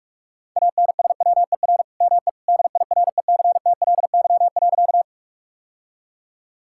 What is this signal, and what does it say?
Morse code, heard as a keyed tone, about 44 wpm, 690 Hz. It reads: ANSWER ME DIRECTLY.
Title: Sentences from Top 1000 Words - Review Part 5 - 45wpm